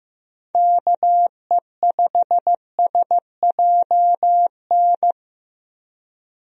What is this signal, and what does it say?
Morse code, heard as a keyed tone, about 15 wpm, 700 Hz. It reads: KE5SJN
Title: USA Call Signs - 6 Characters - Repeated 3x - 15wpm